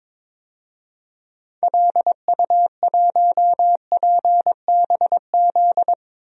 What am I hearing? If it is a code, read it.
Morse code, heard as a keyed tone, about 22 wpm, 700 Hz. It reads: LU1PBZ
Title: DXCC-100 Call Signs - 6 Characters - Repeated 2x - 22wpm